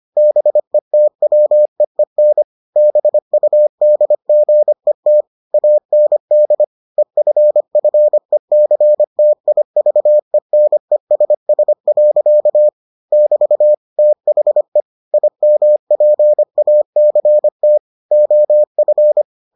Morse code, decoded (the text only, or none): BETWEEN BUDGET AND EFFECTIVENESS. <BT> THE IMPACT OF